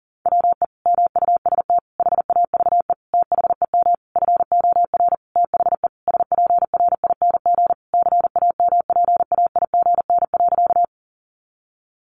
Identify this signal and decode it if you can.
Morse code, heard as a keyed tone, about 40 wpm, 700 Hz. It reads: WE MUST HAVE THEM FOR THE SPRING CAMPAIGN.
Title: Sentences from Top 1200 Words 40wpm